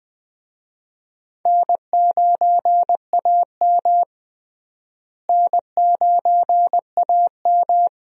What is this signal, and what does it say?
Morse code, heard as a keyed tone, about 20 wpm, 700 Hz. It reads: N9AM N9AM
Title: USA Call Signs - 4 Characters - Repeated 2x - 20wpm